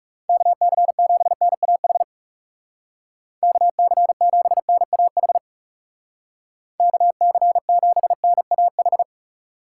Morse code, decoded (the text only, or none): KC7NAH KC7NAH KC7NAH